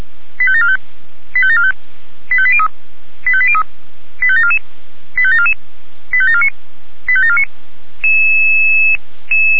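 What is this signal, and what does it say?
Five-tone selective call (ZVEI), heard as quick runs of five tones, tones of about 70 ms. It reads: 85745 85745 85803 85803 85744 85744 85749 85749
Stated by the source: ZVEI five-tone selective call received by SDR.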